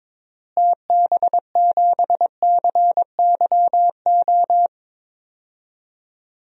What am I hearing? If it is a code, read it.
Morse code, heard as a keyed tone, about 21 wpm, 700 Hz. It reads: TB7CYO
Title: DXCC-100 Call Signs - 6 Characters - Repeated 3x - 22wpm